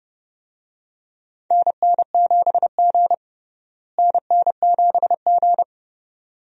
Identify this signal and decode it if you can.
Morse code, heard as a keyed tone, about 30 wpm, 700 Hz. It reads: NN7G NN7G
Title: USA Call Signs - 4 Characters - Repeated 2x - 30wpm